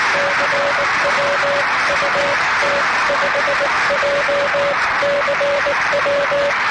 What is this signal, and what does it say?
Morse code, heard as a keyed tone, about 18 wpm, 570 Hz, with noise. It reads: CW UT5JCW